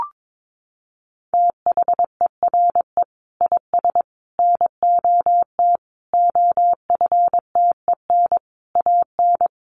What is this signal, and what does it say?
Morse code, heard as a keyed tone, about 22 wpm, 700 Hz. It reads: THERE IS NOT OFTEN AN
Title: Sentences from Top 1400 Words 22wpm